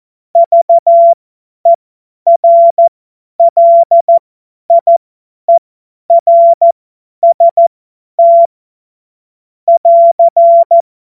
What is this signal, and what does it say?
Morse code, heard as a keyed tone, about 14 wpm, spaced as for 10 wpm, 680 Hz. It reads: VERLIERST <AR>
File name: Morsekurs / Witze / Witz 01. Februar 2022 50 F70